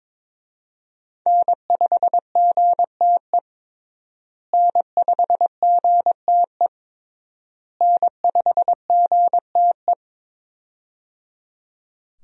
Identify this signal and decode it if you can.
Morse code, heard as a keyed tone, about 21 wpm, 700 Hz. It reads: N5GTE N5GTE N5GTE
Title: USA Call Signs - 5 Characters - Repeated 3x - 22wpm